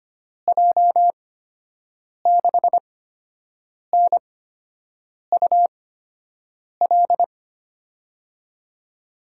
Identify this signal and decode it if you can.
Morse code, heard as a keyed tone, about 25 wpm, spaced as for 7 wpm, 700 Hz. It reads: J6NUL
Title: USA Call Signs - 6 Characters - 8x Character Spacing - 25wpm